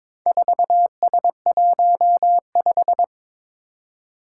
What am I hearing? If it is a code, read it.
Morse code, heard as a keyed tone, about 22 wpm, 700 Hz. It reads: 4S15
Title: Four Challenging Letter-Number Combinations 22wpm